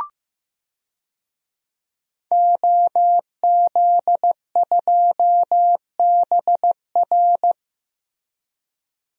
Morse code, heard as a keyed tone, about 15 wpm, 700 Hz. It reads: OZ2BR
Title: DXCC-100 Call Signs - 5 Characters - Repeated 3x - 15wpm